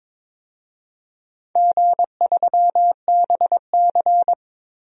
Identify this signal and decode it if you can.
Morse code, heard as a keyed tone, about 22 wpm, 700 Hz. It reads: G3BC
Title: DXCC-100 Call Signs - 4 Characters - Repeated 3x - 22wpm